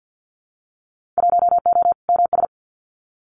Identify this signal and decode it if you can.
Morse code, heard as a keyed tone, about 50 wpm, 700 Hz. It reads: 1O MS
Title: Winter Field Day Exchanges 50wpm